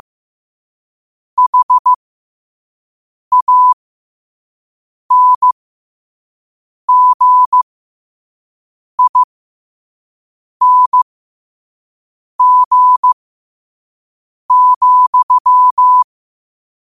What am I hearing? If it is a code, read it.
Morse code, heard as a keyed tone, about 15 wpm, spaced as for 5 wpm, 1000 Hz. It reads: HANGING,
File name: qotdfiles / 5wpm